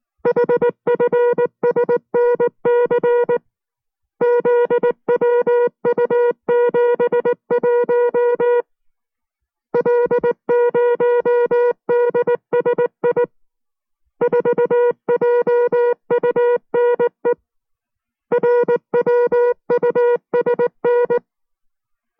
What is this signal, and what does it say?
Morse code, heard as a keyed tone, about 19 wpm, 480 Hz. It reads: HFSNC ZWU71 L0DSI 4JUNE RWUSN